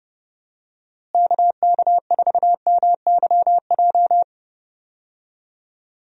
Morse code, heard as a keyed tone, about 30 wpm, 700 Hz. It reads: KK4MYJ
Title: USA Call Signs - 6 Characters - Repeated 2x - 30wpm